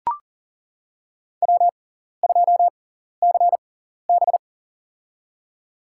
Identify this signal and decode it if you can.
Morse code, heard as a keyed tone, about 40 wpm, spaced as for 14 wpm, 700 Hz. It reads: W2CB
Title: USA Call Signs - 4 Characters - 6x Character Spacing - 40wpm